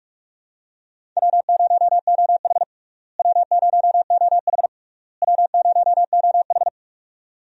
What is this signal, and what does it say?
Morse code, heard as a keyed tone, about 47 wpm, 700 Hz. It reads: W0OH W0OH W0OH
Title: USA Call Signs - 4 Characters - Repeated 3x - 45wpm